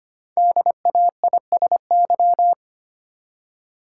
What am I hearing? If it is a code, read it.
Morse code, heard as a keyed tone, about 25 wpm, 700 Hz. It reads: DAISY
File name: practice / 25wpm / Easy Words with Hints - 25wpm